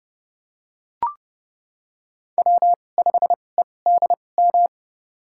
Morse code, heard as a keyed tone, about 29 wpm, spaced as for 22 wpm, 700 Hz. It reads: W5EDM
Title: USA Call Signs - 5 Characters - 2x Character Spacing - 30wpm